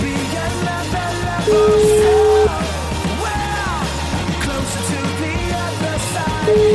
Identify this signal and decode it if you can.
Ringback tone (Europe, 425 Hz), with noise.